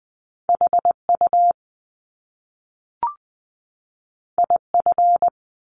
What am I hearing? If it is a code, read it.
Morse code, heard as a keyed tone, about 20 wpm, 700 Hz. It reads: HU IF
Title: Top Two Letter Combinations in Words 20wpm